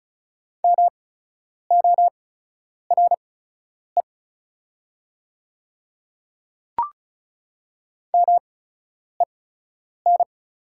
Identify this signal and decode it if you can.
Morse code, heard as a keyed tone, about 36 wpm, spaced as for 10 wpm, 690 Hz. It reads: MORE MEN
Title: 8x Character Spacing - Top 300 Words - 35wpm